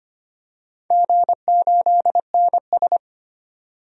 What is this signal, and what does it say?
Morse code, heard as a keyed tone, about 25 wpm, 690 Hz. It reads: G8NS
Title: DXCC-100 Call Signs - 4 Characters - 25wpm